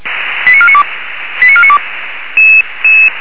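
Five-tone selective call (ZVEI), heard as quick runs of five tones, tones of about 70 ms. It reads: 89492 89492
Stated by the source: Five-tone ZVEI selective call received by SDR.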